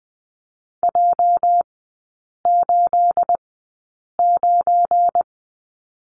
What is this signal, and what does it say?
Morse code, heard as a keyed tone, about 20 wpm, spaced as for 8 wpm, 690 Hz. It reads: J89
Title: 212 Lesson 34e - Number 8 - Three Characters 20wpm